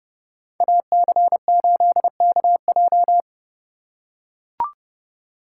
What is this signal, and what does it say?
Morse code, heard as a keyed tone, about 30 wpm, 700 Hz. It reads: AC8KJ
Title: USA Call Signs - 2x Character Spacing - 30wpm